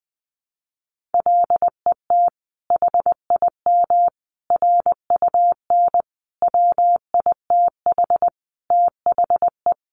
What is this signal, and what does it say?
Morse code, heard as a keyed tone, about 19 wpm, 700 Hz. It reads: LET HIM RUN WITH THE
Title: Sentences from Top 1300 Words 20wpm